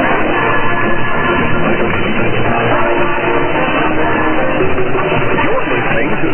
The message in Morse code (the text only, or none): none